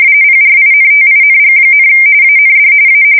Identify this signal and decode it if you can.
RTTY, about 45.45 baud. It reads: PRINTING, AND A
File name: fsk170 / teletype-corp